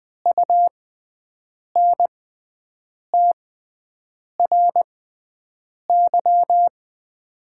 Morse code, heard as a keyed tone, about 20 wpm, spaced as for 7 wpm, 700 Hz. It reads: UNTRY
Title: 6x Character Spacing - Top 500 Words - 20wpm